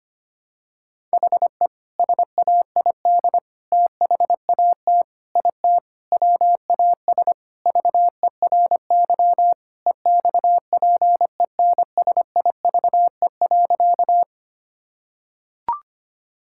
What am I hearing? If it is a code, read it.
Morse code, heard as a keyed tone, about 25 wpm, 700 Hz. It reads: HE SAID THAT IT WAS VERY EXPENSIVE.
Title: Sentences from Top 1000 Words - Review Part 6 - 25wpm